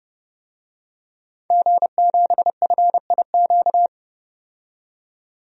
Morse code, heard as a keyed tone, about 30 wpm, 700 Hz. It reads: G7FIQ
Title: DXCC-100 Call Signs - 5 Characters - 30wpm